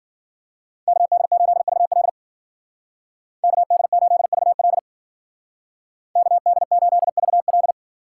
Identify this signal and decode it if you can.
Morse code, heard as a keyed tone, about 53 wpm, 700 Hz. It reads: KD8VL KD8VL KD8VL